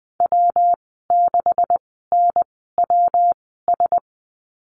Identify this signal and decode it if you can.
Morse code, heard as a keyed tone, about 20 wpm, spaced as for 15 wpm, 700 Hz. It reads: W6NWS
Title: 154 Lesson 25g - Letter p - Call Signs 20wpm